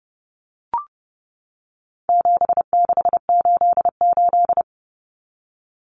Morse code, heard as a keyed tone, about 30 wpm, 700 Hz. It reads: 7688
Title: Common Cut Numbers - 4-Digit 30wpm